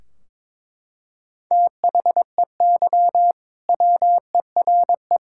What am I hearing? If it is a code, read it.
Morse code, heard as a keyed tone, about 22 wpm, 700 Hz. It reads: THEY WERE
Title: Sentences from Top 1600 Words 22wpm